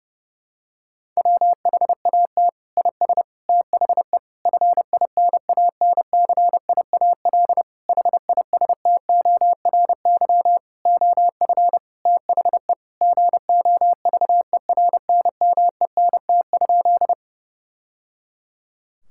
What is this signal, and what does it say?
Morse code, heard as a keyed tone, about 30 wpm, 700 Hz. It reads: WHAT IS THE FINANCIAL HISTORY OF THE GOVERNMENT?
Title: Sentences from Top 1000 Words - Review Part 5 - 30wpm